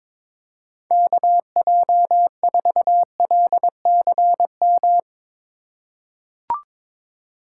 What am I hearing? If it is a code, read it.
Morse code, heard as a keyed tone, about 22 wpm, 700 Hz. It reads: KJ4LCM
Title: USA Call Signs - 6 Characters - 4x Character Spacing - 22wpm